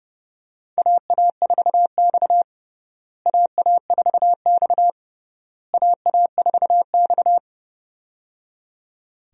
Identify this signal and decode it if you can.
Morse code, heard as a keyed tone, about 30 wpm, 700 Hz. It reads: AA4X AA4X AA4X
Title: USA Call Signs - 4 Characters - Repeated 3x - 30wpm